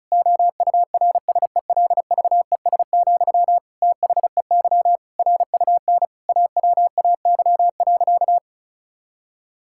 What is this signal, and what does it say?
Morse code, heard as a keyed tone, about 34 wpm, 700 Hz. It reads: OURSELVES, THEY RUN AWAY.
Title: Sentences from Top 1600 Words 35wpm